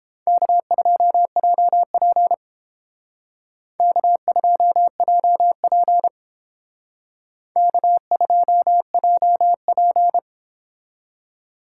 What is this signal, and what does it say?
Morse code, heard as a keyed tone, about 30 wpm, 700 Hz. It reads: K2JP K2JP K2JP